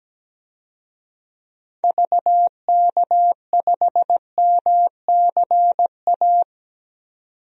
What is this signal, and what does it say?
Morse code, heard as a keyed tone, about 17 wpm, 700 Hz. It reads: VK5MCA
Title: DXCC-100 Call Signs - 6 Characters - 17wpm